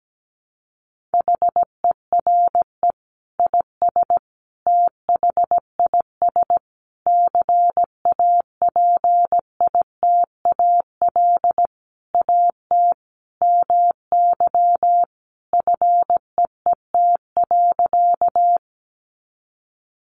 Morse code, heard as a keyed tone, about 17 wpm, 700 Hz. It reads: HERE IS THIS CAPITAL AT MY FEET.
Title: Sentences from Top 1000 Words 17wpm